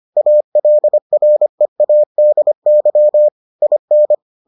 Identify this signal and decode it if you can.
Morse code, heard as a keyed tone, about 25 wpm, 600 Hz. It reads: ALREADY IN